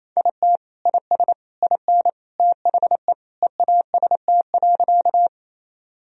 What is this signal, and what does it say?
Morse code, heard as a keyed tone, about 28 wpm, 700 Hz. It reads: IT IS IN THE EAST.
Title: Sentences from Top 1000 Words - Review Part 3 - 28wpm